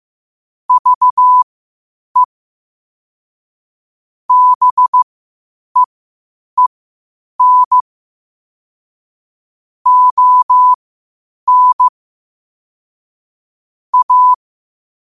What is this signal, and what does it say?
Morse code, heard as a keyed tone, about 15 wpm, spaced as for 8 wpm, 1000 Hz. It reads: VE BEEN ON A
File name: qotdfiles / 8wpm